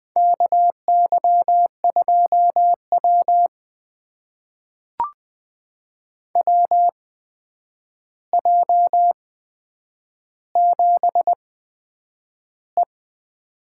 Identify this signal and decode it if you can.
Morse code, heard as a keyed tone, about 20 wpm, spaced as for 9 wpm, 700 Hz. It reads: KY2W WJ7E
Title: USA Call Signs - 4 Characters - 8x Character Spacing - 20wpm